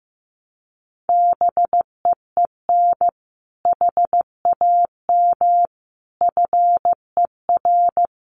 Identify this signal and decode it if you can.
Morse code, heard as a keyed tone, about 16 wpm, 700 Hz. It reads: BEEN HAM FER
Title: Common QSO Phrases Encore 15wpm